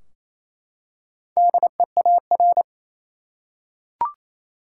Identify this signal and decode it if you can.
Morse code, heard as a keyed tone, about 28 wpm, 700 Hz. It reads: DEAR
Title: Four Letter Words - 28wpm